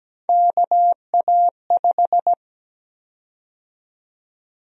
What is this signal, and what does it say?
Morse code, heard as a keyed tone, about 17 wpm, 700 Hz. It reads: KA5
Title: USA Call Sign Prefixes - Repeated 2x - 17wpm